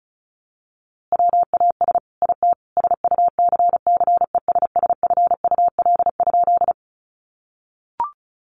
Morse code, heard as a keyed tone, about 34 wpm, 700 Hz. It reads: WAS IT SUCCESSFUL?
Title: A Sound Of Thunder 35wpm